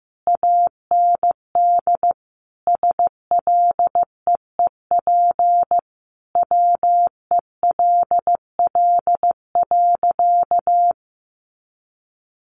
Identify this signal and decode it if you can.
Morse code, heard as a keyed tone, about 15 wpm, 700 Hz. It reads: AND SLEEP WELL.
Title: Sentences from Top 500 Words 15wpm